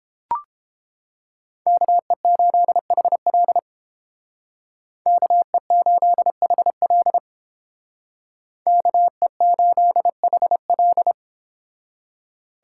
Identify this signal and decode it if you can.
Morse code, heard as a keyed tone, about 30 wpm, 700 Hz. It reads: KE8HL KE8HL KE8HL